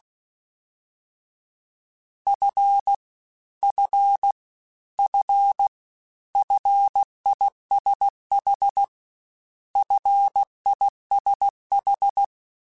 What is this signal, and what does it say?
Morse code, heard as a keyed tone, about 16 wpm, 800 Hz. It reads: F F F FISH FISH
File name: Short Runs / Short Runs-mp3s / SinglePrefixes @15